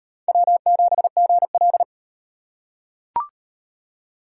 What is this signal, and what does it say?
Morse code, heard as a keyed tone, about 38 wpm, 700 Hz. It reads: W7GL